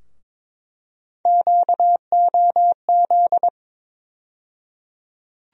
Morse code, heard as a keyed tone, about 22 wpm, 700 Hz. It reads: QOZ
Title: USA Call Sign Postfixes - 3 Letters 22wpm